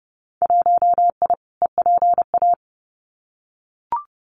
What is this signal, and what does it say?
Morse code, heard as a keyed tone, about 30 wpm, 700 Hz. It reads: 1I EPA
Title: Winter Field Day Exchanges Encore 30wpm